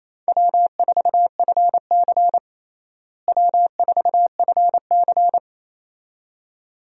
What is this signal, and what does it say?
Morse code, heard as a keyed tone, about 28 wpm, 700 Hz. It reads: W4FC W4FC
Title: USA Call Signs - 4 Characters - Repeated 2x - 28wpm